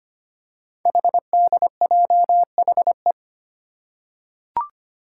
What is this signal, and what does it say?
Morse code, heard as a keyed tone, about 25 wpm, 700 Hz. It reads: HDJHE
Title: Code Groups - 5 Letters - 25wpm